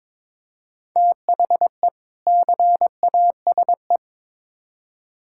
Morse code, heard as a keyed tone, about 22 wpm, 700 Hz. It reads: THE CASE
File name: Build A Phrase - 2-Word Phrases - 22wpm